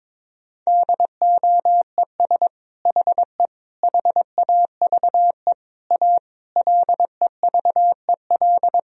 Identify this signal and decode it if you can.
Morse code, heard as a keyed tone, about 22 wpm, 700 Hz. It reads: DOES HE HAVE A LEVEL
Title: Sentences from Top 1000 Words - Review Part 4 - 22wpm